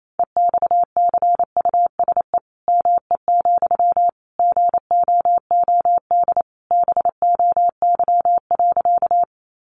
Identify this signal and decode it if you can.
Morse code, heard as a keyed tone, about 28 wpm, 700 Hz. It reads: EXCUSE ME, GOOD BOY.
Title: Sentences from Top 700 Words 28wpm